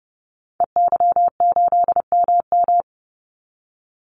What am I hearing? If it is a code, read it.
Morse code, heard as a keyed tone, about 29 wpm, 700 Hz. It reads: EY8MM
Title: Call Signs Repeated 3x - 30wpm